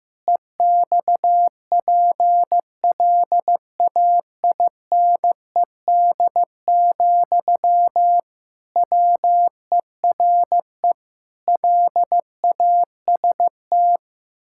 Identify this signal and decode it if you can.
Morse code, heard as a keyed tone, about 15 wpm, 700 Hz. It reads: EXPLAINED, WERE LAST
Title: Pangram Challenge v2 - 15wpm